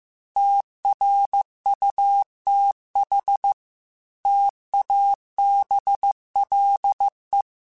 Morse code, heard as a keyed tone, about 15 wpm, 800 Hz. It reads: TRUTH TABLE
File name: Technical Subjects / Technical Subjects-mp3s / AND Gate @13